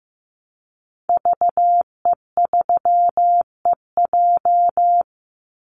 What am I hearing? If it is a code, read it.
Morse code, heard as a keyed tone, about 15 wpm, 700 Hz. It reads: VE3EJ
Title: Call Signs Repeated 3x - 15wpm